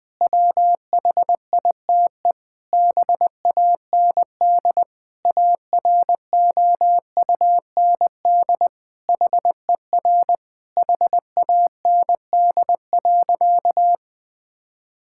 Morse code, heard as a keyed tone, about 20 wpm, 700 Hz. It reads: WHITE BAND AROUND HER HAND.
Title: Sentences from Top 1600 Words 20wpm